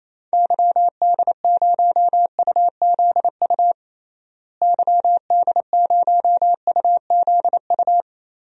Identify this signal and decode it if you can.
Morse code, heard as a keyed tone, about 28 wpm, 700 Hz. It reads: YD0UZU YD0UZU
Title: DXCC-100 Call Signs - 6 Characters - Repeated 3x - 28wpm